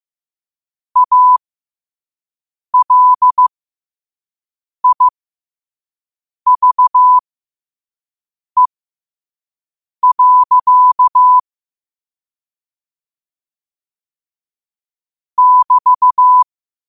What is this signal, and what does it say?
Morse code, heard as a keyed tone, about 15 wpm, spaced as for 5 wpm, 1000 Hz. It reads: ALIVE. <BT>
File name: qotdfiles / 5wpm